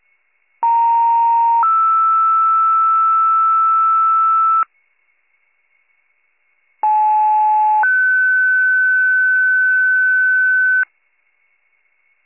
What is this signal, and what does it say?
Two-tone page, heard as a one-second tone, then a three-second tone, about 930 then 1320 Hz, about 860 then 1500 Hz. It is heard from 0.6 to 10.8 s.